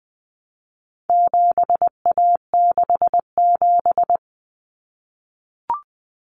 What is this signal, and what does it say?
Morse code, heard as a keyed tone, about 20 wpm, 700 Hz. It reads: 7A67
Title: Common Cut Numbers - 4-Digit 20wpm